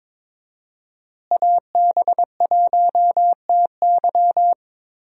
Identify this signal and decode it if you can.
Morse code, heard as a keyed tone, about 22 wpm, 700 Hz. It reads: AB1TY
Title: USA Call Signs - 5 Characters - Repeated 2x - 22wpm